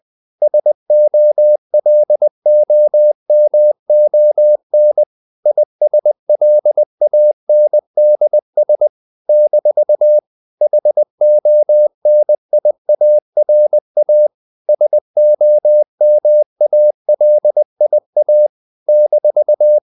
Morse code, heard as a keyed tone, about 20 wpm, 600 Hz. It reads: SOLOMON ISLANDS - HONIARA SOMALIA -
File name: CW / Practice / NationalCapitals20